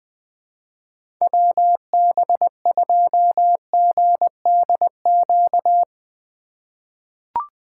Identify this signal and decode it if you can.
Morse code, heard as a keyed tone, about 20 wpm, 700 Hz. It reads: WB2GDQ
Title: USA Call Signs - 6 Characters - 2x Character Spacing - 20wpm